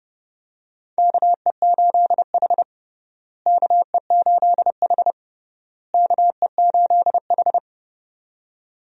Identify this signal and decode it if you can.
Morse code, heard as a keyed tone, about 32 wpm, 700 Hz. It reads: KE8H KE8H KE8H
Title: USA Call Signs - 4 Characters - Repeated 3x - 30wpm